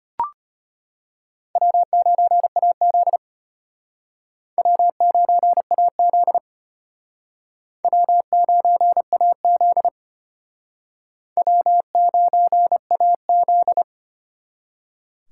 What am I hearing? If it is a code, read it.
Morse code, heard as a keyed tone, about 32 wpm, 700 Hz. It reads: W9AZ W9AZ W9AZ W9AZ